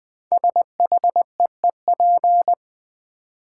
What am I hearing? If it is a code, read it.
Morse code, heard as a keyed tone, about 21 wpm, 700 Hz. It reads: SHEEP
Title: Build A Word - 1 Letter At A Time - 5-Letter Words - 20wpm